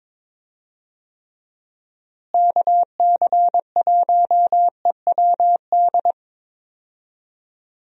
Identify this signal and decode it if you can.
Morse code, heard as a keyed tone, about 22 wpm, 700 Hz. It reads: KC1EWD
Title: USA Call Signs - 6 Characters - Repeated 2x - 22wpm